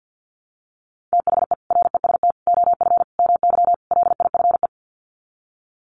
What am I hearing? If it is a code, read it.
Morse code, heard as a keyed tone, about 50 wpm, 700 Hz. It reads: THE REST OF MY LIFE
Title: Sets of 5 Words 50wpm